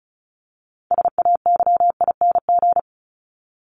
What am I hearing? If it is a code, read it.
Morse code, heard as a keyed tone, about 34 wpm, 700 Hz. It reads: SAYING
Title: ING Suffix - Top 100 Words with Base Words 35wpm